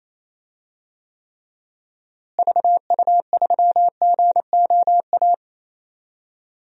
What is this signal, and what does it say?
Morse code, heard as a keyed tone, about 28 wpm, 700 Hz. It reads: VU3GOA
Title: DXCC-100 Call Signs - 6 Characters - Repeated 2x - 28wpm